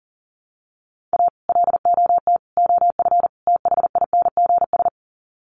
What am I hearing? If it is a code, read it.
Morse code, heard as a keyed tone, about 40 wpm, 700 Hz. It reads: A LOT OF THINGS
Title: Sets of 4 Words 40wpm